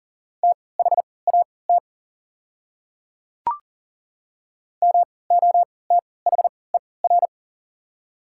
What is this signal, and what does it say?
Morse code, heard as a keyed tone, about 40 wpm, spaced as for 23 wpm, 700 Hz. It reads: THAT MOTHER
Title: 3x Character Spacing - Top 300 Words - 40wpm